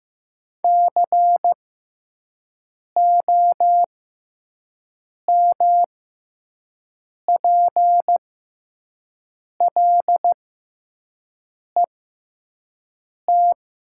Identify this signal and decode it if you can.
Morse code, heard as a keyed tone, about 15 wpm, spaced as for 5 wpm, 700 Hz. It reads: COMPLET
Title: 6x Character Spacing - Top 500 Words - 15wpm